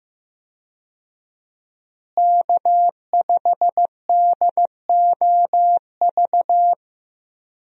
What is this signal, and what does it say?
Morse code, heard as a keyed tone, about 15 wpm, 700 Hz. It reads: K5DOV
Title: USA Call Signs - 5 Characters - Repeated 3x - 15wpm